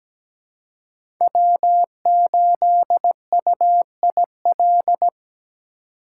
Morse code, heard as a keyed tone, about 17 wpm, 700 Hz. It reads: W8UIL
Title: K1USN SST QSO Practice - 17wpm